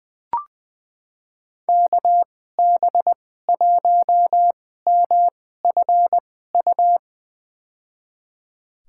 Morse code, heard as a keyed tone, about 20 wpm, spaced as for 14 wpm, 700 Hz. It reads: KB1MFU
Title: USA Call Signs - 6 Characters - 2x Character Spacing - 20wpm